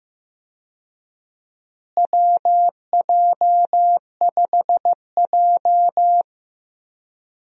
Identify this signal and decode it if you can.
Morse code, heard as a keyed tone, about 15 wpm, 700 Hz. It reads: WJ5J
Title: USA Call Signs - 4 Characters - Repeated 2x - 15wpm